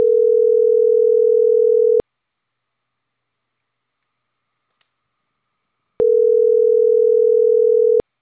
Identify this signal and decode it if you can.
Ringback tone (North America), 440 + 480 Hz.